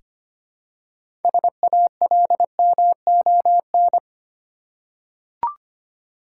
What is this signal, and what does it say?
Morse code, heard as a keyed tone, about 25 wpm, 700 Hz. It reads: SALMON
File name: practice / 25wpm / Easy Words with Hints - 25wpm